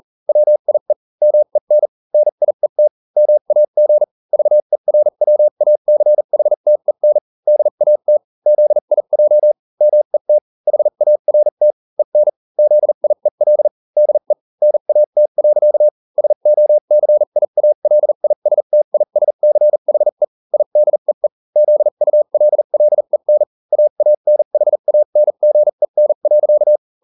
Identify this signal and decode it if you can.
Morse code, heard as a keyed tone, about 40 wpm, spaced as for 35 wpm, 600 Hz. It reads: WIE MEN NIET MAG VERWACHTEN DAT ZIJ MET HART EN ZIEL DE NAT. SOCIALISTISCHE IDEE ZULLEN AANHANGEN.